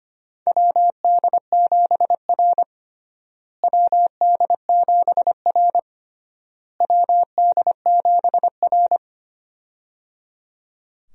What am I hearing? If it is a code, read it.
Morse code, heard as a keyed tone, about 25 wpm, 700 Hz. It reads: WD7R WD7R WD7R
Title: USA Call Signs - 4 Characters - Repeated 3x - 25wpm